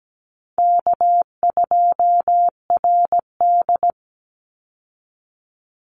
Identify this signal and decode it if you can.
Morse code, heard as a keyed tone, about 17 wpm, 700 Hz. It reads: K2RD
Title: Call Signs - 17wpm